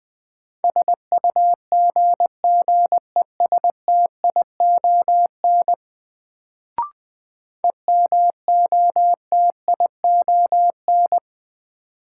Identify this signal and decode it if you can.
Morse code, heard as a keyed tone, about 20 wpm, 700 Hz. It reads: SUGGESTION EMOTION
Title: Sentences from Top 1600 Words - New Word Drill - 20wpm